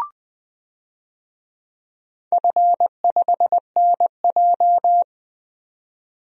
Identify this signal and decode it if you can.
Morse code, heard as a keyed tone, about 20 wpm, 700 Hz. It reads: F5NJ
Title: DXCC-100 Call Signs - 4 Characters - 20wpm